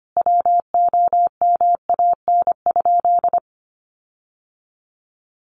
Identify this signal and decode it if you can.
Morse code, heard as a keyed tone, about 25 wpm, 700 Hz. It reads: WOMAN?
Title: Sentences from Top 1400 Words 25wpm